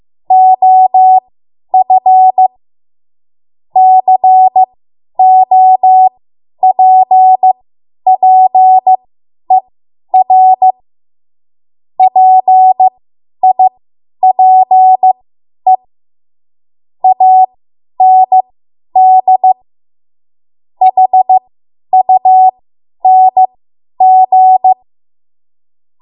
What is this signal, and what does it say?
Morse code, heard as a keyed tone, about 15 wpm, spaced as for 10 wpm, 750 Hz. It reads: OF COPPER PIPE AND HUNG
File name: Morse / 10WPM